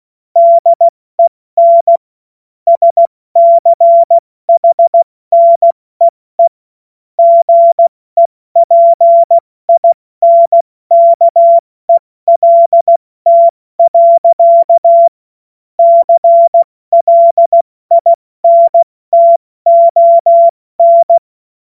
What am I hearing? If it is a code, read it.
Morse code, heard as a keyed tone, about 16 wpm, 680 Hz. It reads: DEN SCHNEE GEPINKELT. CLINTON
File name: Morsekurs/Witze